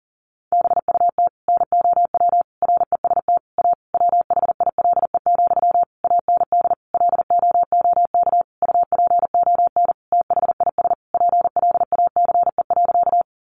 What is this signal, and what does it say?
Morse code, heard as a keyed tone, about 40 wpm, 700 Hz. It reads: BUT NOW REST A WHILE, AND LOOK UPON THIS PLACE.
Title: Sentences from Top 1100 Words 40wpm